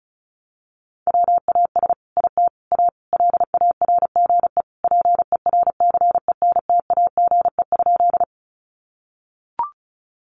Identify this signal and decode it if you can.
Morse code, heard as a keyed tone, about 34 wpm, 700 Hz. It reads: WAS IT A LARGE PERCENTAGE?
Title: Sentences from Top 1300 Words 35wpm